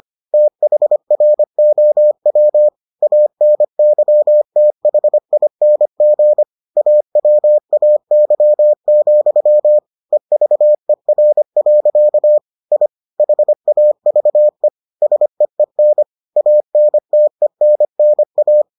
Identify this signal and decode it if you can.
Morse code, heard as a keyed tone, about 25 wpm, 600 Hz. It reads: THROW ANYTHING AWAY, EVER. I HAVE SEEN ANTENNA